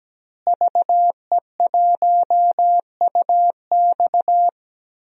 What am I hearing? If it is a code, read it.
Morse code, heard as a keyed tone, about 17 wpm, 700 Hz. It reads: VE1UX
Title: DXCC-100 Call Signs - 5 Characters - Repeated 3x - 17wpm